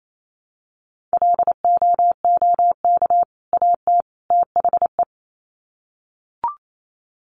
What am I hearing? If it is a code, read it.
Morse code, heard as a keyed tone, about 28 wpm, 700 Hz. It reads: LOOK AT THE
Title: Sets of 3 Words 28wpm